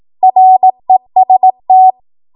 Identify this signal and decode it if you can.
Morse code, heard as a keyed tone, about 18 wpm, 750 Hz. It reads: REST
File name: Morse / 18WPM